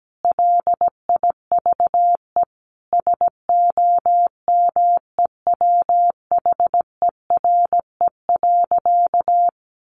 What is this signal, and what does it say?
Morse code, heard as a keyed tone, about 17 wpm, 700 Hz. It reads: LIVE SOMEWHERE.
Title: Sentences from Top 1400 Words 17wpm